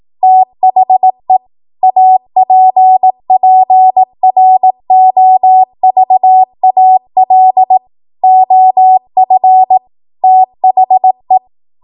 Morse code, heard as a keyed tone, about 18 wpm, 750 Hz. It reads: THE APPROVAL OF THE
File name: Morse / 18WPM